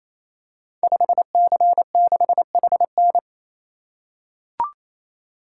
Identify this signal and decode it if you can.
Morse code, heard as a keyed tone, about 27 wpm, 700 Hz. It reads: 5C6HN
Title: Code Groups - 5 Letter:Numbers - 28wpm